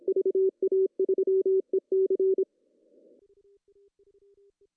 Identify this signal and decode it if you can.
Morse code, heard as a keyed tone, about 26 wpm, 380 Hz. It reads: VA3EC
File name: CW / cqww-cw-k5tr